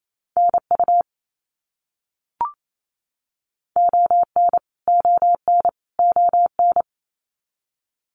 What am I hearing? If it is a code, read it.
Morse code, heard as a keyed tone, about 28 wpm, 700 Hz. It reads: NU ON ON ON
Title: Canadian Provinces and Territories - Repeated 3x - 28wpm